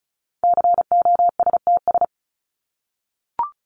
Morse code, heard as a keyed tone, about 34 wpm, 700 Hz. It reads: COSTS
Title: ES:S Suffix - Top 100 Words with Base Words 35wpm